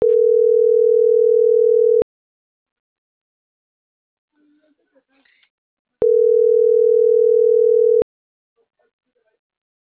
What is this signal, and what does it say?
Ringback tone (North America), 440 + 480 Hz.